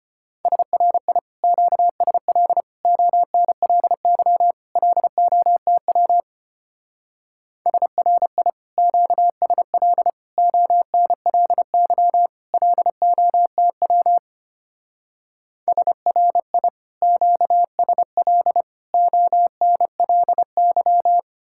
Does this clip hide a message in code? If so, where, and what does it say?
Yes: Morse code from 0.5 to 21.2 s, heard as a keyed tone, about 32 wpm, 700 Hz. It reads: SRI QSL ONLY LOTW SRI QSL ONLY LOTW SRI QSL ONLY